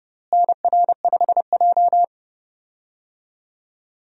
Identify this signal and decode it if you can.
Morse code, heard as a keyed tone, about 30 wpm, 700 Hz. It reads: NR5J
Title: USA Call Signs - 4 Characters - Repeated 2x - 30wpm